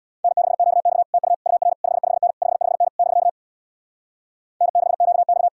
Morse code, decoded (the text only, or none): N6QX TU UR 55N 55N <BK> N6QX